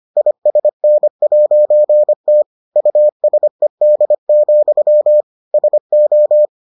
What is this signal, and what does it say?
Morse code, heard as a keyed tone, about 25 wpm, 600 Hz. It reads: ISN'T USED, SO